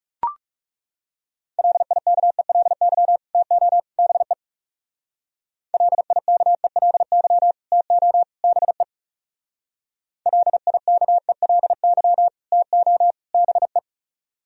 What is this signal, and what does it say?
Morse code, heard as a keyed tone, about 37 wpm, 700 Hz. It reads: LIKELY TO BE LIKELY TO BE LIKELY TO BE
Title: Speed-Racing - Sets of 3 Words 30wpm